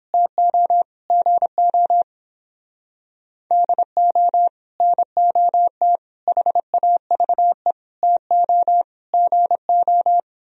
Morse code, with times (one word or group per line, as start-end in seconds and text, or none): 0.1-0.8 TO
1.1-2.0 GO
3.5-4.5 DO
4.8-6.0 NOT
6.3-7.7 HAVE
8.0-8.8 TO
9.1-10.2 GO